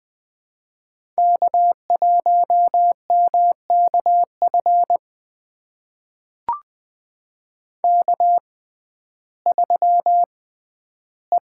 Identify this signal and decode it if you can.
Morse code, heard as a keyed tone, about 20 wpm, spaced as for 12 wpm, 700 Hz. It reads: K1MKF K3E
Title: USA Call Signs - 5 Characters - 6x Character Spacing - 20wpm